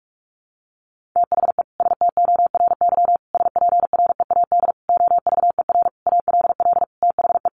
Morse code, heard as a keyed tone, about 44 wpm, 700 Hz. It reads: THE STORY SPREAD OVER ALL THE
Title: Sentences from Top 1200 Words 45wpm